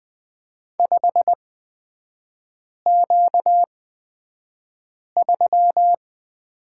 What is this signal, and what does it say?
Morse code, heard as a keyed tone, about 20 wpm, spaced as for 5 wpm, 700 Hz. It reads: 5Q3
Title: 236 Lesson 38e - Letter q - Three Characters 20wpm